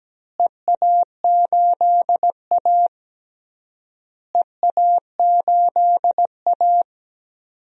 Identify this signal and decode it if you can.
Morse code, heard as a keyed tone, about 17 wpm, 700 Hz. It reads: EA8A EA8A
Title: DXCC-100 Call Signs - 4 Characters - Repeated 3x - 17wpm